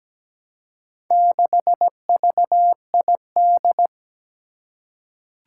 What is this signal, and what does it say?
Morse code, heard as a keyed tone, about 17 wpm, 700 Hz. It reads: 6VID
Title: Four Challenging Letter-Number Combinations 17wpm